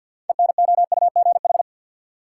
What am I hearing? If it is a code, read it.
Morse code, heard as a keyed tone, about 53 wpm, 700 Hz. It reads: ENOUGH
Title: Sentences from Top 400 Words - New Word Drill - 50wpm